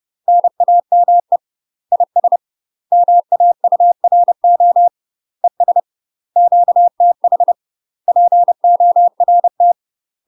Morse code, heard as a keyed tone, about 30 wpm, 700 Hz. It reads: NAME IS MAURO ES QTH PORT
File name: UK MASTER QSOs TRAINING / 30WPM / UK MASTER QSOs 0024